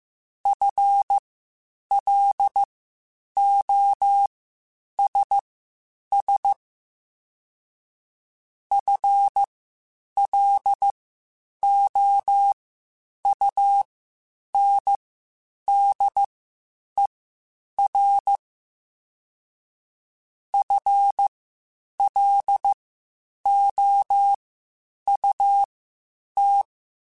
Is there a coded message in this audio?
Yes: Morse code from 0.5 to 26.6 s, heard as a keyed tone, about 15 wpm, spaced as for 8 wpm, 800 Hz. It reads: FLOSS FLOUNDER FLOUT